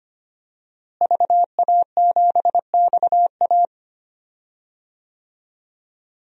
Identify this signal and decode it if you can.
Morse code, heard as a keyed tone, about 25 wpm, 700 Hz. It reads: VA7XA
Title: DXCC-100 Call Signs - 5 Characters - Repeated 3x - 25wpm